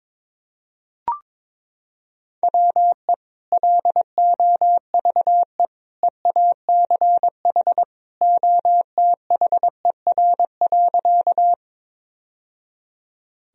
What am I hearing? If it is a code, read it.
Morse code, heard as a keyed tone, about 22 wpm, 700 Hz. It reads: WE LOVE EACH OTHER.
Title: Sentences from Top 1000 Words - Review Part 1 - 22wpm